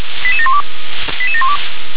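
Five-tone selective call (ZVEI), heard as quick runs of five tones, tones of about 70 ms. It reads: 88712 88712
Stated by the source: ZVEI five-tone selective call received on an SDR.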